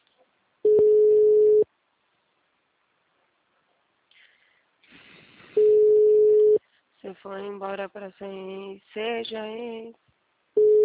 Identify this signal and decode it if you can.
Ringback tone (Europe, 425 Hz), with speech.
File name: out- -2019- - - .1237